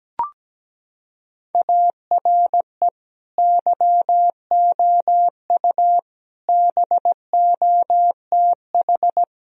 Morse code, heard as a keyed tone, about 17 wpm, 700 Hz. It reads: ARE YOU BOTH
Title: Sentences from Top 1000 Words - Review Part 1 - 17wpm